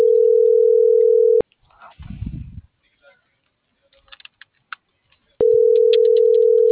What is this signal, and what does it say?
Ringback tone (North America), 440 + 480 Hz.